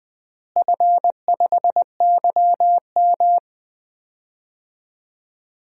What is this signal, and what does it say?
Morse code, heard as a keyed tone, about 20 wpm, 700 Hz. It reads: F5YM
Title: DXCC-100 Call Signs - 4 Characters - Repeated 2x - 20wpm